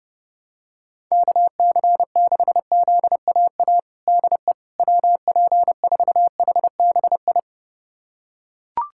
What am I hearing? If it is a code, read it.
Morse code, heard as a keyed tone, about 30 wpm, 700 Hz. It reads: KC6ZAA DE WP4HBI
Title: USA Callsign DE Callsign - 6 Characters - 30wpm